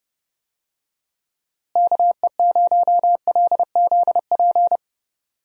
Morse code, heard as a keyed tone, about 32 wpm, 700 Hz. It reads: KE0LZP
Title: USA Call Signs - 6 Characters - Repeated 3x - 30wpm